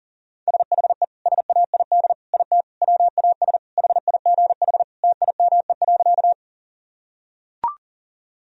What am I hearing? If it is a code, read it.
Morse code, heard as a keyed tone, about 40 wpm, 700 Hz. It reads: SHE SAID IT WAS HIGH TIME.
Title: Sentences from Top 1000 Words - Review Part 1 - 40wpm